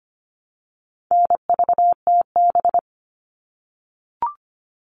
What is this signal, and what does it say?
Morse code, heard as a keyed tone, about 25 wpm, 700 Hz. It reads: NVTB
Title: All Cut Numbers - 4-Digit 25wpm